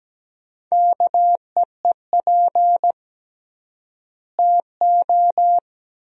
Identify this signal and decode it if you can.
Morse code, heard as a keyed tone, about 17 wpm, 700 Hz. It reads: KEEP TO
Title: 3x Word-Spacing - Sentences from Top 300 Words - 17wpm